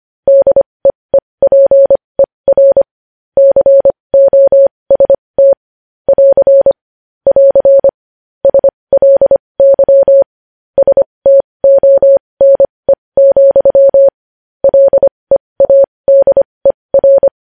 Morse code, heard as a keyed tone, about 25 wpm, spaced as for 20 wpm, 550 Hz. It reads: DEEPER COST <AR> <AR> SLY STONE, LEADER
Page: news-headlines-in-morse-code-at-20-wpm